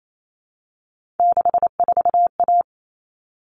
Morse code, heard as a keyed tone, about 28 wpm, 700 Hz. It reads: 64A
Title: All Cut Numbers - 3-Digit 28wpm